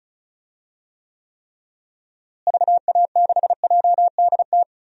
Morse code, heard as a keyed tone, about 34 wpm, 700 Hz. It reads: VA6JDT
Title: DXCC-100 Call Signs - 6 Characters - Repeated 3x - 35wpm